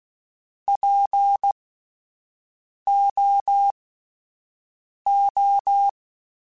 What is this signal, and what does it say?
Morse code, heard as a keyed tone, about 16 wpm, spaced as for 5 wpm, 800 Hz. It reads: POO